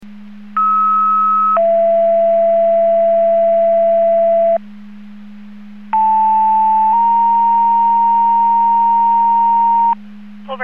Two-tone page, heard as a one-second tone, then a three-second tone, about 1290 then 670 Hz, about 880 then 930 Hz. It is heard from 0.6 to 9.9 s.
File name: Portage / 00148 17-31 Portage 154.385 Paging